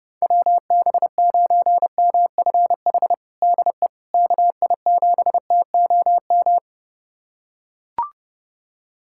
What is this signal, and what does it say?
Morse code, heard as a keyed tone, about 30 wpm, 700 Hz. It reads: WB9MFH DE KI7TOM
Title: USA Callsign DE Callsign - 6 Characters - 30wpm